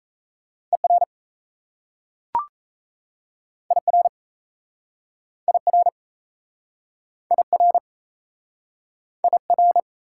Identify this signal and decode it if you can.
Morse code, heard as a keyed tone, about 38 wpm, 700 Hz. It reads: ER IR IR IR IR